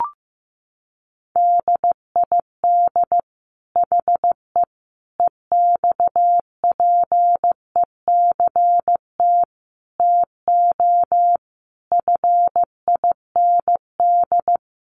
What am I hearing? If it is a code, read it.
Morse code, heard as a keyed tone, about 14 wpm, 700 Hz. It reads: DID HE EXPECT TO FIND
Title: Sentences from Top 1100 Words 15wpm